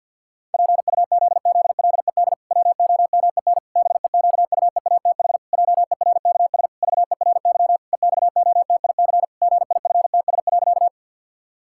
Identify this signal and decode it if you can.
Morse code, heard as a keyed tone, about 50 wpm, 700 Hz. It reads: PUZZLED WOMEN BEQUEATH JERKS VERY EXOTIC GIFTS.
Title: Pangram Challenge v2 - 50wpm